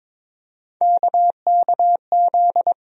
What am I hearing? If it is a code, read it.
Morse code, heard as a keyed tone, about 22 wpm, 700 Hz. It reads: KKZ
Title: USA Call Sign Postfixes - 3 Letters 22wpm